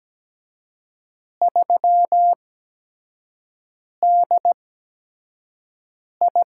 Morse code, heard as a keyed tone, about 17 wpm, spaced as for 5 wpm, 700 Hz. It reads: 3DI
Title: USA Call Signs - 4 Characters - 8x Character Spacing - 17wpm